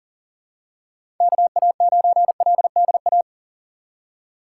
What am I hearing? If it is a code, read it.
Morse code, heard as a keyed tone, about 39 wpm, 700 Hz. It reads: KA9LDA
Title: USA Call Signs - 4x Character Spacing - 40wpm